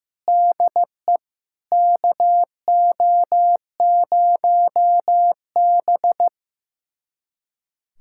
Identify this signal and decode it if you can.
Morse code, heard as a keyed tone, about 15 wpm, 700 Hz. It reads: DE KO0B
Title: USA Callsign DE Callsign - 4 Characters - 15wpm